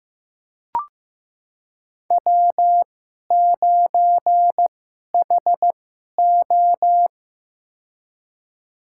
Morse code, heard as a keyed tone, about 15 wpm, spaced as for 11 wpm, 700 Hz. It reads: W9HO
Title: USA Call Signs - 4 Characters - 2x Character Spacing - 15wpm